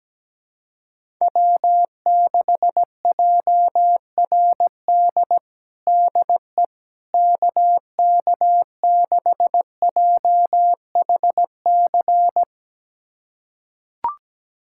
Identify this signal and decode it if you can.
Morse code, heard as a keyed tone, about 17 wpm, 700 Hz. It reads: W6JRD DE KK6JHC
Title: USA Callsign DE Callsign 17wpm